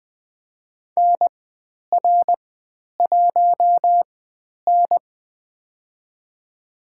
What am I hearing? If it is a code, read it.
Morse code, heard as a keyed tone, about 20 wpm, spaced as for 10 wpm, 700 Hz. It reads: NR1N
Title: 095 Lesson 16g - Letter c - Call Signs 20wpm